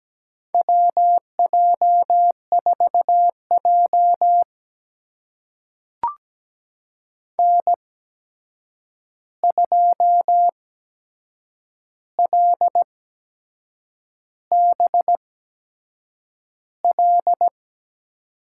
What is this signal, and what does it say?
Morse code, heard as a keyed tone, about 17 wpm, spaced as for 7 wpm, 700 Hz. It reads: WJ4J N2LBL
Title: USA Call Signs - 8x Character Spacing - 17wpm